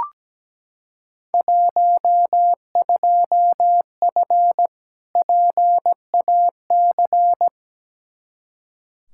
Morse code, heard as a keyed tone, about 17 wpm, 700 Hz. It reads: 12F PAC
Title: Field Day Exchanges Encore - 17wpm